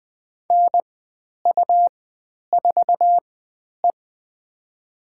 Morse code, heard as a keyed tone, about 20 wpm, spaced as for 10 wpm, 700 Hz. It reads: NU4E
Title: 109 Lesson 18g - Letter m - Call Signs 20wpm